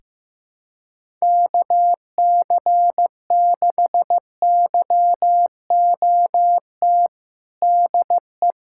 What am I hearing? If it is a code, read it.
Morse code, heard as a keyed tone, about 15 wpm, 700 Hz. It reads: KC6YOT DE
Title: USA Callsign DE Callsign 15wpm